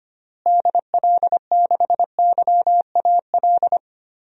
Morse code, heard as a keyed tone, about 25 wpm, 700 Hz. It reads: DL6YAL
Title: DXCC-100 Call Signs - 6 Characters - Repeated 3x - 25wpm